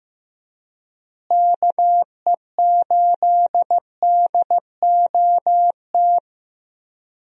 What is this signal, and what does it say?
Morse code, heard as a keyed tone, about 15 wpm, 700 Hz. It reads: KE8DOT
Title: USA Call Signs - 6 Characters - Repeated 3x - 15wpm